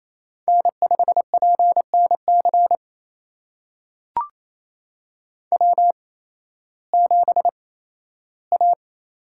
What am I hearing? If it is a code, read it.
Morse code, heard as a keyed tone, about 28 wpm, spaced as for 15 wpm, 700 Hz. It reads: N5PNC W7A
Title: USA Call Signs - 5 Characters - 8x Character Spacing - 28wpm